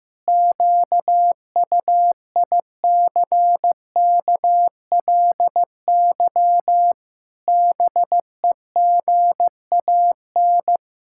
Morse code, heard as a keyed tone, about 15 wpm, 700 Hz. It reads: QUICKLY BEGAN
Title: Pangram Challenge v2 - 15wpm